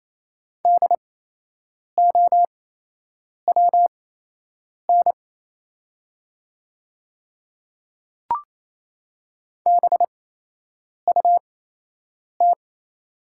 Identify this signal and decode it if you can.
Morse code, heard as a keyed tone, about 28 wpm, spaced as for 8 wpm, 700 Hz. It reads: DOWN BUT
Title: 8x Character Spacing - Top 100 Words - 28wpm